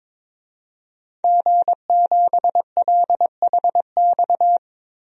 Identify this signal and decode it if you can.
Morse code, heard as a keyed tone, about 22 wpm, 700 Hz. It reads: G7LHX
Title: DXCC-100 Call Signs - 5 Characters - Repeated 3x - 22wpm